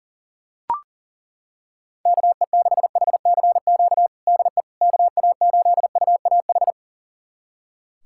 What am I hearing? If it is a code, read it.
Morse code, heard as a keyed tone, about 40 wpm, 700 Hz. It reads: KE6HCQ DE KA8UAH
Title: USA Callsign DE Callsign 40wpm